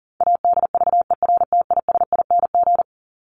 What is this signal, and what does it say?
Morse code, heard as a keyed tone, about 38 wpm, 700 Hz. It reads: ADVERTISING
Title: Top 751-1000 Words 40wpm